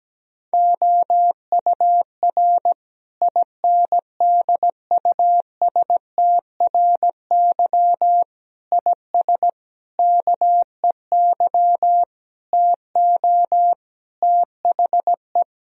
Morse code, heard as a keyed tone, about 17 wpm, 700 Hz. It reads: OUR INDUSTRY IS KEY TO THE
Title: Sentences from Top 1000 Words - Review Part 4 - 17wpm